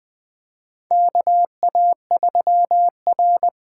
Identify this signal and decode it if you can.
Morse code, heard as a keyed tone, about 20 wpm, 700 Hz. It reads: KA3R
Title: USA Call Signs - 4 Characters - 8x Character Spacing - 20wpm